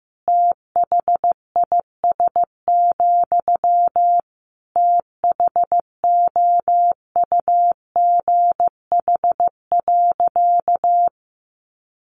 Morse code, heard as a keyed tone, about 15 wpm, 700 Hz. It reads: THIS, THOUGH.
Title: Sentences from Top 1400 Words 15wpm